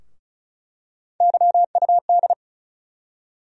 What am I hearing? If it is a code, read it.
Morse code, heard as a keyed tone, about 36 wpm, 700 Hz. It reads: YUD
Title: USA Call Sign Postfixes - 3 Letters 35wpm